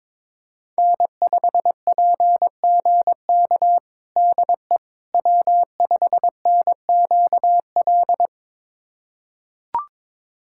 Morse code, heard as a keyed tone, about 21 wpm, 700 Hz. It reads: N5PGK DE W5NQL
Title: USA Callsign DE Callsign - 5 Characters - 22wpm